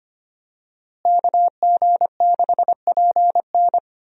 Morse code, heard as a keyed tone, about 25 wpm, 700 Hz. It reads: KG6PN
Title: USA Call Signs - 5 Characters - Repeated 3x - 25wpm